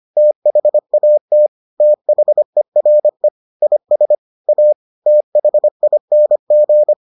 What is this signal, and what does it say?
Morse code, heard as a keyed tone, about 25 wpm, 600 Hz. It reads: THAT THERE IS A THING